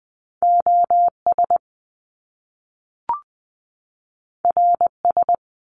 Morse code, heard as a keyed tone, about 20 wpm, 700 Hz. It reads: OS RS
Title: Top Two Letter Combinations in Words 20wpm